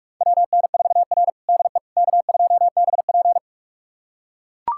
Morse code, heard as a keyed tone, about 44 wpm, 690 Hz. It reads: WN4R DE K2BP
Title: USA Callsign DE Callsign - 4 Characters - 45wpm